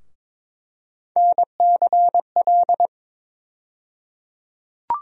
Morse code, heard as a keyed tone, about 22 wpm, 700 Hz. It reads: NCL
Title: USA Call Sign Postfixes - 3 Letters 22wpm